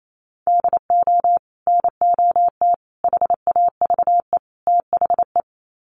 Morse code, heard as a keyed tone, about 28 wpm, 700 Hz. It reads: DO NOT HAVE THE
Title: Sets of 4 Words 28wpm